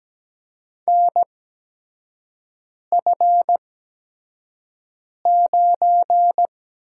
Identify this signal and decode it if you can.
Morse code, heard as a keyed tone, about 17 wpm, spaced as for 5 wpm, 700 Hz. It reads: NF9
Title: USA Call Signs - 8x Character Spacing - 17wpm